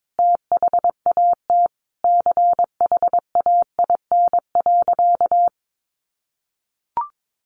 Morse code, heard as a keyed tone, about 22 wpm, 700 Hz. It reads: THAT CHAIN.
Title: Sentences from Top 1500 Words 22wpm